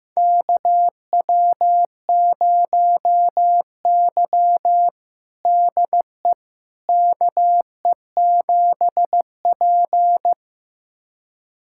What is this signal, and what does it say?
Morse code, heard as a keyed tone, about 15 wpm, 700 Hz. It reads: KW0Y DE KE7P
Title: USA Callsign DE Callsign - 4 Characters - 15wpm